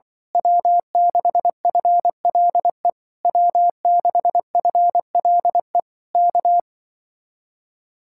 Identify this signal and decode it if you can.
Morse code, heard as a keyed tone, about 24 wpm, 700 Hz. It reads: W6FLE W6FLE K
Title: Speed-Racing Common QSO Phrases - 20wpm